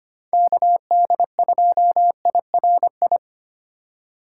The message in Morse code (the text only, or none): KD2IRI